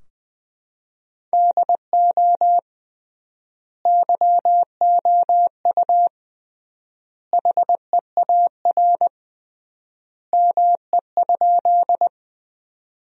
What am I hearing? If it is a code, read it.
Morse code, heard as a keyed tone, about 20 wpm, 700 Hz. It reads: DO YOU HEAR ME?
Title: 3x Word-Spacing - Sentences from Top 300 Words - 20wpm